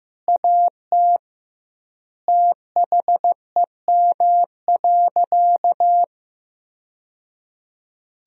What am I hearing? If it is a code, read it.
Morse code, heard as a keyed tone, about 15 wpm, 700 Hz. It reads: AT THEM.
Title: 2x Word-Spacing - Sentences from Top 100 Words - 15wpm